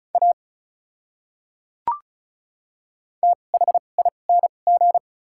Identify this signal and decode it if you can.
Morse code, heard as a keyed tone, about 34 wpm, spaced as for 25 wpm, 700 Hz. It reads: A THING
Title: 2x Character Spacing - Top 100 Words - 35wpm